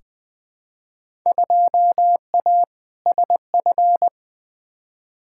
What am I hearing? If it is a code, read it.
Morse code, heard as a keyed tone, about 20 wpm, 700 Hz. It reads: 2A SF
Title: Field Day Exchanges - 20wpm